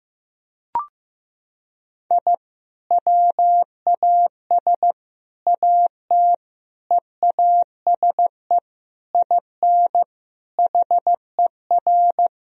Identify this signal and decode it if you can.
Morse code, heard as a keyed tone, about 15 wpm, 700 Hz. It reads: I WAS AT EASE IN HER
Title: Sentences from Top 1000 Words - Review Part 1 - 15wpm